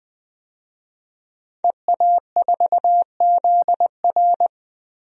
Morse code, heard as a keyed tone, about 21 wpm, 700 Hz. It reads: EA4ZR
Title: DXCC-100 Call Signs - 5 Characters - 20wpm